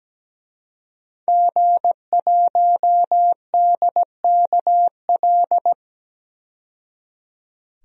Morse code, heard as a keyed tone, about 17 wpm, 700 Hz. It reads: G1DKL
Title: DXCC-100 Call Signs - 5 Characters - 17wpm